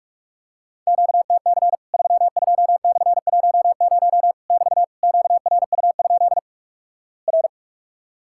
Morse code, heard as a keyed tone, about 48 wpm, 700 Hz. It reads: QTC 32/10 <BT> QRU? R
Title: Worked All Europe DX Contest - QTC Exchange - 45wpm